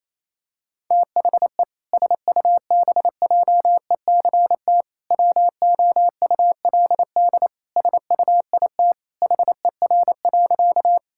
Morse code, heard as a keyed tone, about 28 wpm, 700 Hz. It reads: THE SUBJECT WOULD SUIT HER.
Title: Sentences from Top 1600 Words 28wpm